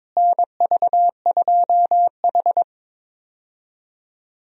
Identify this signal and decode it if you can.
Morse code, heard as a keyed tone, about 22 wpm, 700 Hz. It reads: NV2H
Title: USA Call Signs - 4 Characters - Repeated 2x - 22wpm